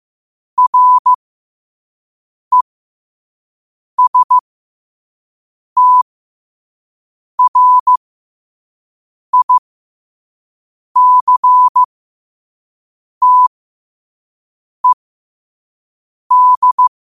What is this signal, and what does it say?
Morse code, heard as a keyed tone, about 15 wpm, spaced as for 5 wpm, 1000 Hz. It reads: RESTRICTED